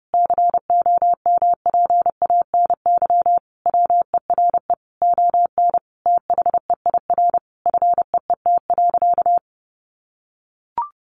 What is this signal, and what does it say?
Morse code, heard as a keyed tone, about 30 wpm, 700 Hz. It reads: COMPANY WERE ON THEIR FEET.
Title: Sentences from Top 1000 Words 30wpm